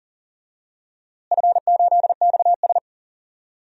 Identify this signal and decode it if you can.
Morse code, heard as a keyed tone, about 40 wpm, 700 Hz. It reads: F8XS
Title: DXCC-100 Call Signs - 4 Characters - Repeated 3x - 40wpm